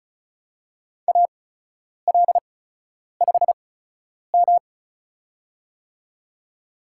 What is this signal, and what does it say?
Morse code, heard as a keyed tone, about 36 wpm, spaced as for 10 wpm, 700 Hz. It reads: AL5M
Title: USA Call Signs - 8x Character Spacing - 35wpm